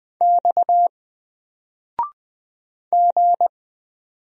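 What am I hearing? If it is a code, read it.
Morse code, heard as a keyed tone, about 20 wpm, 700 Hz. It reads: X G
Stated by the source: Single letter and number Morse drill at 20 wpm, from N4LCD.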